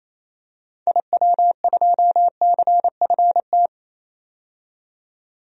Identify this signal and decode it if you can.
Morse code, heard as a keyed tone, about 28 wpm, 700 Hz. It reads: IW2CFT
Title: DXCC-100 Call Signs - 6 Characters - 28wpm